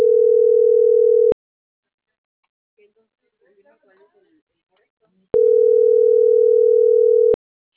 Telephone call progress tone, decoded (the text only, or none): ringback tone